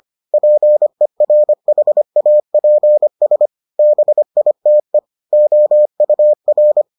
Morse code, heard as a keyed tone, about 25 wpm, 600 Hz. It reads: PERHAPS BITE OUR